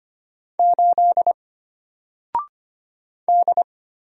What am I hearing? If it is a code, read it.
Morse code, heard as a keyed tone, about 25 wpm, 700 Hz. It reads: 8 D
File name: Single-Letter-Number-25wpm